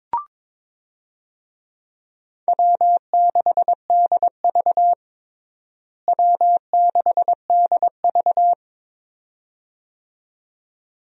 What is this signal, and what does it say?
Morse code, heard as a keyed tone, about 22 wpm, 700 Hz. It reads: W6DV W6DV
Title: USA Call Signs - 4 Characters - Repeated 2x - 22wpm